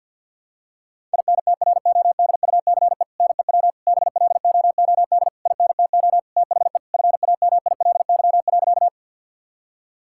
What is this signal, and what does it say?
Morse code, heard as a keyed tone, about 51 wpm, 700 Hz. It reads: INTRODUCE NEW BLOOD INTO THE FAMILY.
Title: Sentences from Top 1600 Words 50wpm